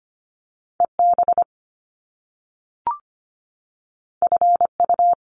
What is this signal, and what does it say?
Morse code, heard as a keyed tone, about 25 wpm, 700 Hz. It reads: EB FU
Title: Top Two Letter Combinations in Words 25wpm